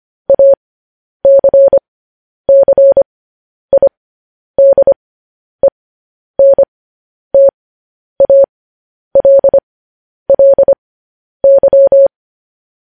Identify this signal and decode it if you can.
Morse code, heard as a keyed tone, about 25 wpm, spaced as for 10 wpm, 550 Hz. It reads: ACCIDENTALLY
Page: news headlines in morse code at 10 wpm